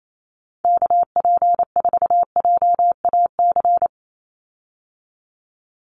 Morse code, heard as a keyed tone, about 28 wpm, 700 Hz. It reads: KP4JAC
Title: USA Call Signs 28wpm